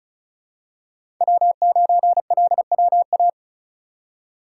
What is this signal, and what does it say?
Morse code, heard as a keyed tone, about 36 wpm, 700 Hz. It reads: W9LWA
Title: USA Call Signs - 5 Characters - 3x Character Spacing - 35wpm